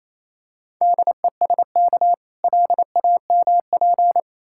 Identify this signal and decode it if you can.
Morse code, heard as a keyed tone, about 26 wpm, 700 Hz. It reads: DESK LAMP
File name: Easy Words - Speed-Racing with Hints - 22wpm